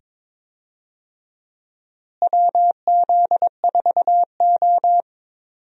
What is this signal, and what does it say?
Morse code, heard as a keyed tone, about 22 wpm, 700 Hz. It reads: WZ4O
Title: USA Call Signs - 4 Characters - Repeated 2x - 22wpm